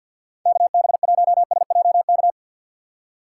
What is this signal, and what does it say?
Morse code, heard as a keyed tone, about 47 wpm, 700 Hz. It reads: KB1SJK
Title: USA Call Signs - 6 Characters - 6x Character Spacing - 50wpm